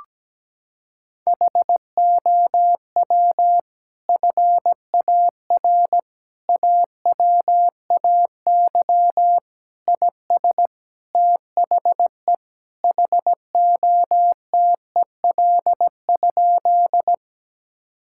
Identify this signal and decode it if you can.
Morse code, heard as a keyed tone, about 17 wpm, 700 Hz. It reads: HOW FAR AWAY IS THE HOTEL?
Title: Sentences from Top 1600 Words 17wpm